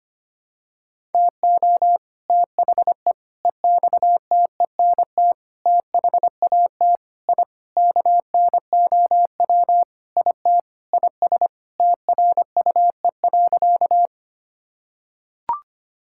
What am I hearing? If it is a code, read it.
Morse code, heard as a keyed tone, about 25 wpm, 700 Hz. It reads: TO THE EXTENT THAT I KNOW IT IS TRUE.
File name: practice / 25wpm / Sentences with 4-Word Phrases - 25wpm